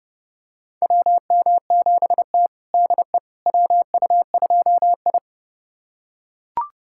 Morse code, heard as a keyed tone, about 32 wpm, 700 Hz. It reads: WM7T DE WU2I
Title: USA Callsign DE Callsign - 4 Characters - 30wpm